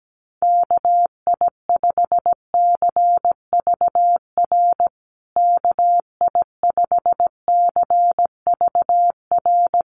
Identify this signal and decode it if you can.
Morse code, heard as a keyed tone, about 17 wpm, 700 Hz. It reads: KI5CVR KI5CVR
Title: USA Call Signs Repeated 2x 17wpm